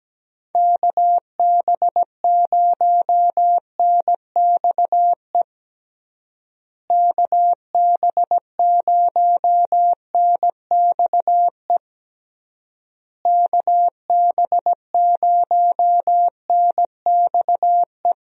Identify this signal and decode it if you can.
Morse code, heard as a keyed tone, about 17 wpm, 700 Hz. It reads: KB0NXE KB0NXE KB0NXE
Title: USA Call Signs - 6 Characters - Repeated 3x - 17wpm